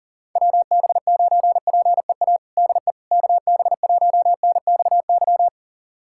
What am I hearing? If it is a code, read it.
Morse code, heard as a keyed tone, about 40 wpm, 700 Hz. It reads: WB9PEA DE KB1NXY
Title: USA Callsign DE Callsign - 6 Characters - 40wpm